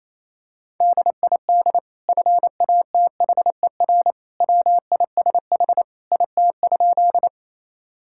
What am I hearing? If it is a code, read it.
Morse code, heard as a keyed tone, about 28 wpm, 700 Hz. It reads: DID FATHER WISH IT?
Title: Sentences from Top 1600 Words 28wpm